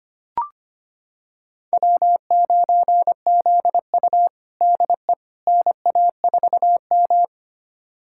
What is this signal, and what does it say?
Morse code, heard as a keyed tone, about 25 wpm, 700 Hz. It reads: W9ZU DE NA4M
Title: USA Callsign DE Callsign - 4 Characters - 25wpm